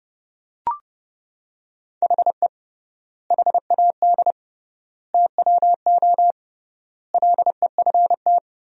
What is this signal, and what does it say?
Morse code, heard as a keyed tone, about 30 wpm, 700 Hz. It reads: HE HAD TWO LEFT
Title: 3x Word-Spacing - Sentences from Top 300 Words - 30wpm